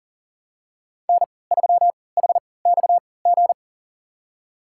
Morse code, heard as a keyed tone, about 40 wpm, spaced as for 23 wpm, 700 Hz. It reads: N3HXG
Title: USA Call Signs - 5 Characters - 3x Character Spacing - 40wpm